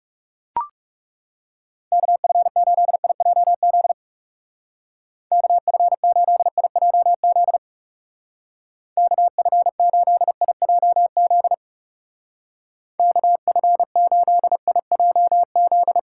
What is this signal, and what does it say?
Morse code, heard as a keyed tone, about 37 wpm, 700 Hz. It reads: KF8IJZ KF8IJZ KF8IJZ KF8IJZ
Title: Speed Racing - USA 2x3 Call Signs 30wpm